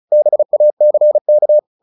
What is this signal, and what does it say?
Morse code, heard as a keyed tone, about 35 wpm, 600 Hz. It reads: BACK
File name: the old man and the sea 2 35wpm